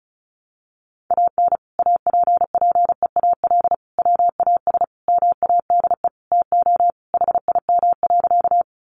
Morse code, heard as a keyed tone, about 34 wpm, 700 Hz. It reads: AN APPEAL WAS MADE TO HIM.
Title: Sentences from Top 1300 Words 35wpm